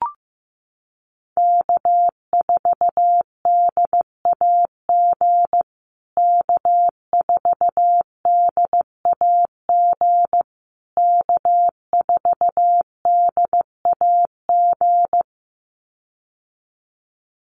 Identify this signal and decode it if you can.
Morse code, heard as a keyed tone, about 14 wpm, 700 Hz. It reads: K4DAG K4DAG K4DAG
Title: USA Call Signs Repeated 3x 15wpm